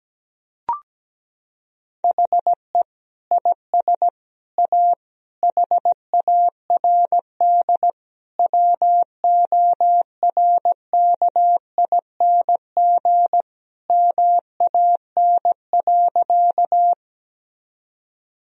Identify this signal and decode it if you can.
Morse code, heard as a keyed tone, about 17 wpm, 700 Hz. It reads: HE IS A HARD WORKING MAN.
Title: Sentences from Top 1000 Words - Review Part 4 - 17wpm